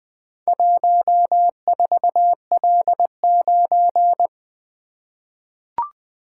Morse code, heard as a keyed tone, about 20 wpm, 700 Hz. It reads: 14L9
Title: Four Letter-Number Combinations - 20wpm